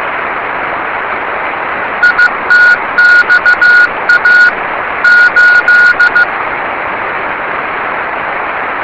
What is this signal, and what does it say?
Morse code, heard as a keyed tone, about 15 wpm, 1450 Hz. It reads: ITXA 8
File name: yubileiry cw by dk3wn